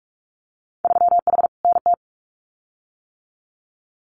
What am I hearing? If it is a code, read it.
Morse code, heard as a keyed tone, about 46 wpm, 700 Hz. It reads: 3H NT
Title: Winter Field Day Exchanges Encore 45wpm